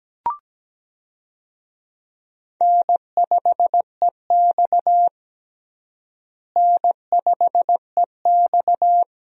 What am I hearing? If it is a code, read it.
Morse code, heard as a keyed tone, about 17 wpm, 700 Hz. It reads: N5EX N5EX
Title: USA Call Signs - 4 Characters - Repeated 2x - 17wpm